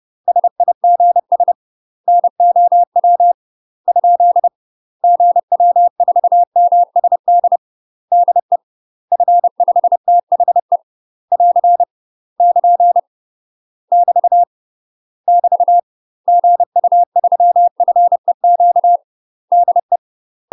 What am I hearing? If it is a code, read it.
Morse code, heard as a keyed tone, about 30 wpm, 700 Hz. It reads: SIGS NOW ? GW4MSD DE F5THE <AR> <KN> <BT> <BT> GU3FEQ DE
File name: UK MASTER QSOs TRAINING / 30WPM / UK MASTER QSOs 0001